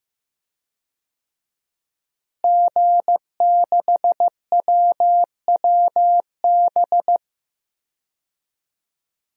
Morse code, heard as a keyed tone, about 15 wpm, 700 Hz. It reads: G6WWB
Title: DXCC-100 Call Signs - 5 Characters - Repeated 2x - 15wpm